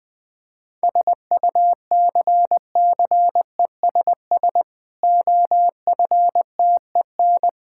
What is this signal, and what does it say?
Morse code, heard as a keyed tone, about 20 wpm, 700 Hz. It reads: SUCCESS OFTEN
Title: Difficult Sentences with Hints - 20wpm